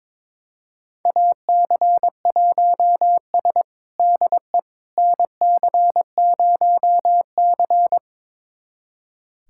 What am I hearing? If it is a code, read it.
Morse code, heard as a keyed tone, about 22 wpm, 700 Hz. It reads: AC1S DE NC0C
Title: USA Callsign DE Callsign - 4 Characters - 22wpm